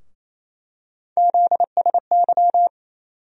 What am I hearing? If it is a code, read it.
Morse code, heard as a keyed tone, about 27 wpm, 700 Hz. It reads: ZSY
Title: USA Call Sign Postfixes - 3 Letters 28wpm